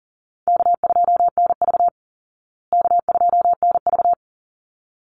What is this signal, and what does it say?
Morse code, heard as a keyed tone, about 40 wpm, 700 Hz. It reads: K2NV K2NV
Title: Easy Call Signs Repeated 2x - 40wpm